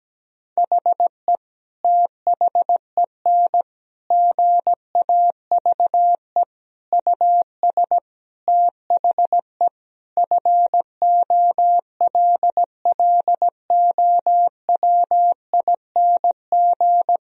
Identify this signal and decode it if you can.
Morse code, heard as a keyed tone, about 17 wpm, 700 Hz. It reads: HE THEN GAVE US THE FOLLOWING
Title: Sentences from Top 1000 Words - Review Part 4 - 17wpm